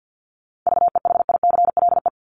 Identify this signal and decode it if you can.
Morse code, heard as a keyed tone, about 50 wpm, 700 Hz. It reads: VEHICLE
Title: Top 1751-2000 Words 50wpm